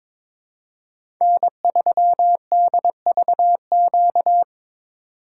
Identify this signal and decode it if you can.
Morse code, heard as a keyed tone, about 22 wpm, 700 Hz. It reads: N3DVQ
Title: USA Call Signs - 5 Characters - 2x Character Spacing - 22wpm